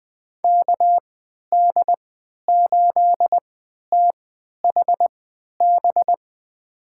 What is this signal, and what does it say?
Morse code, heard as a keyed tone, about 20 wpm, spaced as for 11 wpm, 700 Hz. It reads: KD8THB
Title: USA Call Signs - 6 Characters - 3x Character Spacing - 20wpm